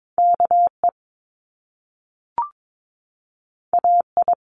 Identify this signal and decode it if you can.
Morse code, heard as a keyed tone, about 22 wpm, 700 Hz. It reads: KE AI
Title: Top Two Letter Combinations in Words 22wpm